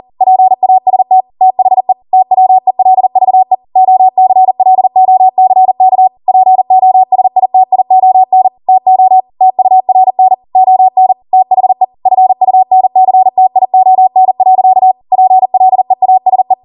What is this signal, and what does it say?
Morse code, heard as a keyed tone, about 40 wpm, 750 Hz. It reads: PAST THE TWELVE OCLOCK POSITION TO TURN ON THE FUNCTION. PLEASE
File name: Morse / 40WPM